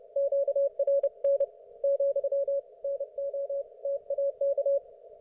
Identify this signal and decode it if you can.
Morse code, heard as a keyed tone, about 30 wpm, 560 Hz. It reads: QRN , NO TAK
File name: Archiv / CUC208 / ok1fhi cul208